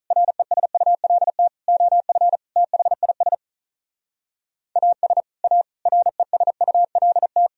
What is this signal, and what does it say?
Morse code, heard as a keyed tone, about 45 wpm, 700 Hz. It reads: RESULT OF THIS AS A RESULT